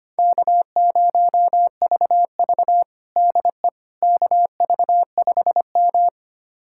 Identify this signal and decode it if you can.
Morse code, heard as a keyed tone, about 25 wpm, 700 Hz. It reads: K0VV DE KV5M
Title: USA Callsign DE Callsign - 4 Characters - 25wpm